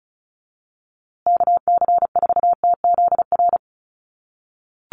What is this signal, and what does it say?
Morse code, heard as a keyed tone, about 34 wpm, 700 Hz. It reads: KC4TZR
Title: USA Call Signs Repeated 3x 35wpm